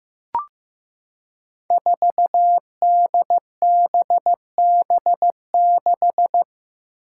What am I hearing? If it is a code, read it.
Morse code, heard as a keyed tone, about 15 wpm, 700 Hz. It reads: 4DBB6
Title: Five Challenging Letter-Number Combinations 15wpm